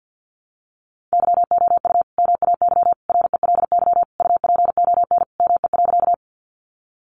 Morse code, heard as a keyed tone, about 50 wpm, 700 Hz. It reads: YOU MAY RELY UPON ME.
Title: Sentences from Top 1200 Words 50wpm